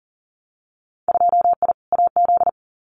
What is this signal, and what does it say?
Morse code, heard as a keyed tone, about 40 wpm, 700 Hz. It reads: 2I AZ
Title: Winter Field Day Exchanges 40wpm